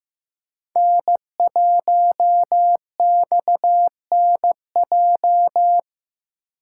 Morse code, heard as a keyed tone, about 15 wpm, 700 Hz. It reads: N1XNJ
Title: USA Call Signs - 5 Characters - 6x Character Spacing - 15wpm